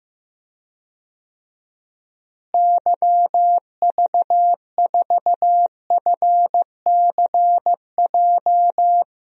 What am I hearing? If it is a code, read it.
Morse code, heard as a keyed tone, about 15 wpm, 700 Hz. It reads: YV4FCJ
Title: DXCC-100 Call Signs - 6 Characters - Repeated 3x - 15wpm